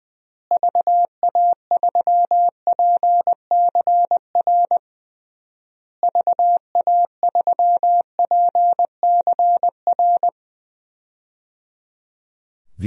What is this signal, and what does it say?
Morse code, heard as a keyed tone, about 20 wpm, 700 Hz. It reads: VA3PCR VA3PCR
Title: DXCC-100 Call Signs - 6 Characters - Repeated 3x - 20wpm